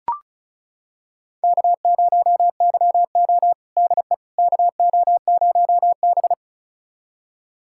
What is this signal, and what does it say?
Morse code, heard as a keyed tone, about 34 wpm, 700 Hz. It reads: K0YO DE KO0B
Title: USA Callsign DE Callsign - 4 Characters - 35wpm